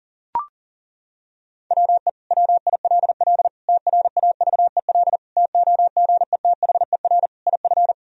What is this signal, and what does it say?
Morse code, heard as a keyed tone, about 40 wpm, 700 Hz. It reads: WE WILL TRAVEL TOGETHER IF
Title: Sentences from Top 1000 Words - Review Part 2 - 40wpm